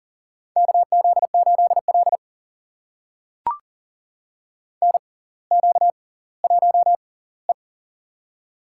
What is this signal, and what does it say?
Morse code, heard as a keyed tone, about 40 wpm, spaced as for 20 wpm, 700 Hz. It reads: KZ8L NQ1E
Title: USA Call Signs - 6x Character Spacing - 40wpm